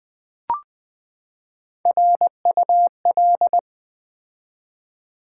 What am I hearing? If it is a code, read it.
Morse code, heard as a keyed tone, about 20 wpm, 700 Hz. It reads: RUL
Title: Build A Word - 3 Letters At A Time - 6-Letter Words - 20wpm